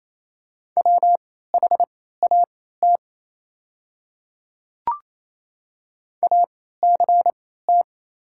Morse code, heard as a keyed tone, about 28 wpm, spaced as for 16 wpm, 700 Hz. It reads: WHAT ACT
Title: 3x Character Spacing - Top 200 Words - 28wpm